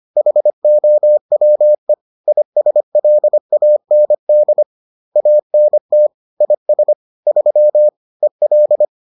Morse code, heard as a keyed tone, about 25 wpm, 600 Hz. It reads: HOWE ISLAND ANT IS 3 EL
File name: 25wpm-6